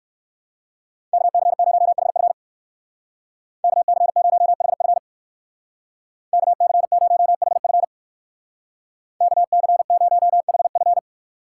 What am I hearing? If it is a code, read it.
Morse code, heard as a keyed tone, about 57 wpm, 700 Hz. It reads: KC0HF KC0HF KC0HF KC0HF